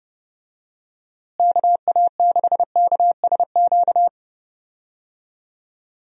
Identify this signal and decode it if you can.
Morse code, heard as a keyed tone, about 32 wpm, 700 Hz. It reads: KA6KSQ
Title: USA Call Signs - 6 Characters - Repeated 3x - 30wpm